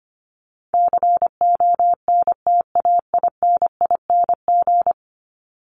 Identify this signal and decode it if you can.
Morse code, heard as a keyed tone, about 25 wpm, 700 Hz. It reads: CONTAINING
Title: ING Suffix - Top 100 Words with Base Words 25wpm